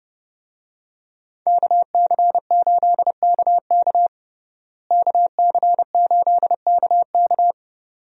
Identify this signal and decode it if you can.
Morse code, heard as a keyed tone, about 30 wpm, 700 Hz. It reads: KC8KK KC8KK
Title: USA Call Signs - 5 Characters - Repeated 2x - 30wpm